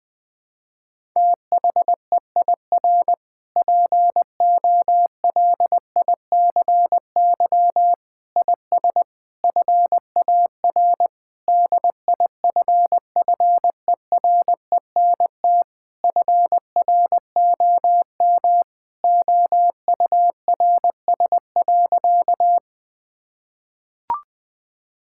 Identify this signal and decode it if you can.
Morse code, heard as a keyed tone, about 20 wpm, 700 Hz. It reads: THEIR POLICY IS FAR DIFFERENT FROM OURS.
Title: Sentences from Top 1000 Words - Review Part 5 - 20wpm